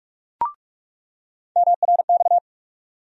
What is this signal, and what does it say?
Morse code, heard as a keyed tone, about 41 wpm, 700 Hz. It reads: MRX
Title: USA Call Sign Postfixes - 3 Letters 45wpm